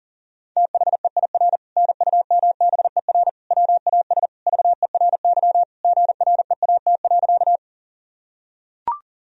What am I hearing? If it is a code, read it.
Morse code, heard as a keyed tone, about 37 wpm, 700 Hz. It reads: THEIR NUMBER WAS VERY GREAT.
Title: Sentences from Top 1000 Words - Review Part 1 - 40wpm